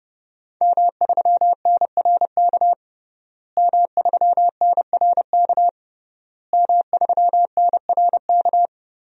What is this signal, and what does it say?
Morse code, heard as a keyed tone, about 30 wpm, 700 Hz. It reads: M3NRK M3NRK M3NRK
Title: DXCC-100 Call Signs - 5 Characters - Repeated 3x - 30wpm